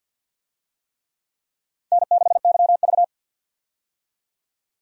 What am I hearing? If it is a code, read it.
Morse code, heard as a keyed tone, about 52 wpm, 700 Hz. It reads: N6YV
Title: USA Call Signs - 4 Characters - 50wpm